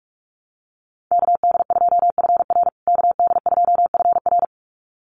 Morse code, heard as a keyed tone, about 44 wpm, 690 Hz. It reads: KD2FR KD2FR
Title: USA Call Signs Repeated 2x 45wpm